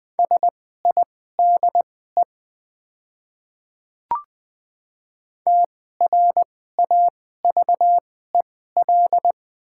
Morse code, heard as a keyed tone, about 20 wpm, spaced as for 14 wpm, 700 Hz. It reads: SIDE TRAVEL
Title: 2x Character Spacing - Top 500 Words - 20wpm